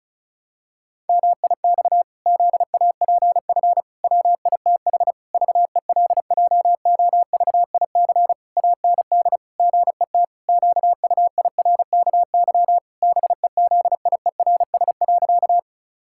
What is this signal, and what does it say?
Morse code, heard as a keyed tone, about 36 wpm, 700 Hz. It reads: MIX ZAPF WITH VELJOVIC AND GET QUIRKY BEZIERS.
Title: Pangram Challenge v2 - 35wpm